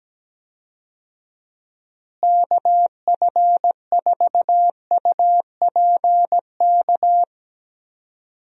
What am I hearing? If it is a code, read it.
Morse code, heard as a keyed tone, about 17 wpm, 700 Hz. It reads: KF4UPK
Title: USA Call Signs - 6 Characters - Repeated 3x - 17wpm